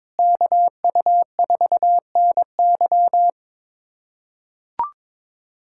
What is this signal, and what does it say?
Morse code, heard as a keyed tone, about 22 wpm, 700 Hz. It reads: KU4NY
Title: USA Callsign DE Callsign - 5 Characters - 22wpm